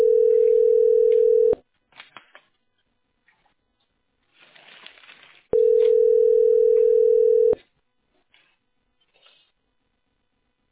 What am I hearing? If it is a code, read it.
Ringback tone (North America), 440 + 480 Hz.